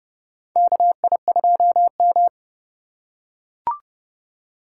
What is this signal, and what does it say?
Morse code, heard as a keyed tone, about 30 wpm, 700 Hz. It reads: KI2M
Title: USA Call Signs - 2x Character Spacing - 30wpm